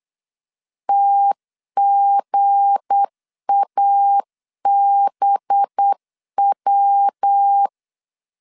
Morse code, heard as a keyed tone, about 8 wpm, 800 Hz. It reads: TGABW
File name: LEZIONI / 7 GF9Z / 7 GF 8wpm 3108